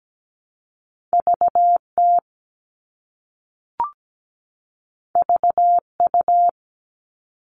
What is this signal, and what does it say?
Morse code, heard as a keyed tone, about 17 wpm, 700 Hz. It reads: VT VU
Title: All Cut Numbers - 2-Digit 17wpm